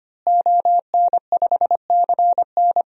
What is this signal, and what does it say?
Morse code, heard as a keyed tone, about 25 wpm, 700 Hz. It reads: ON5CN
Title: DXCC-100 Call Signs - 5 Characters - Repeated 3x - 25wpm